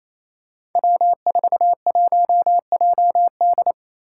Morse code, heard as a keyed tone, about 28 wpm, 700 Hz. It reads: W41JD
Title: Five Challenging Letter-Number Combinations 28wpm